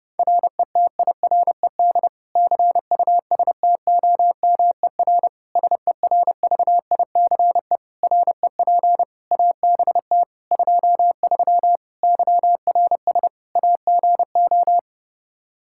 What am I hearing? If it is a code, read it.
Morse code, heard as a keyed tone, about 30 wpm, 700 Hz. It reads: RETIRED CUSTOMER SERVICE REP ABT 23 YRS AGO
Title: K1USN SST QSO Practice 30wpm